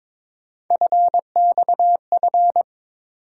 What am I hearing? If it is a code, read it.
Morse code, heard as a keyed tone, about 22 wpm, 700 Hz. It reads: FXF
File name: practice / 22wpm / Three Letter-Number Combinations - 22wpm